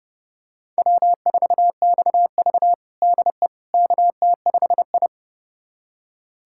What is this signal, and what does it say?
Morse code, heard as a keyed tone, about 30 wpm, 700 Hz. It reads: W4XV DE KT5I
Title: USA Callsign DE Callsign - 4 Characters - 30wpm